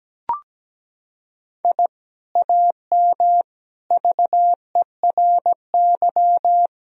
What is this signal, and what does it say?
Morse code, heard as a keyed tone, about 17 wpm, 700 Hz. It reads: I AM VERY
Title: Sentences from Top 1000 Words - Review Part 3 - 17wpm